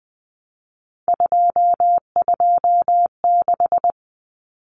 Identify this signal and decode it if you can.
Morse code, heard as a keyed tone, about 20 wpm, 700 Hz. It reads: 226
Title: Numbers - 3-Digit 20wpm